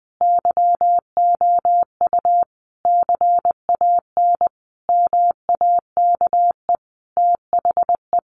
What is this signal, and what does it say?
Morse code, heard as a keyed tone, about 20 wpm, 700 Hz. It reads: YOU CAN MAKE THE
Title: Sentences from Top 1400 Words 20wpm